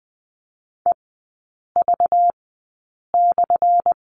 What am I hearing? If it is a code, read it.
Morse code, heard as a keyed tone, about 20 wpm, spaced as for 8 wpm, 690 Hz. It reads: EV/
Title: 191 Lesson 31e - Letter v - Three Characters 20wpm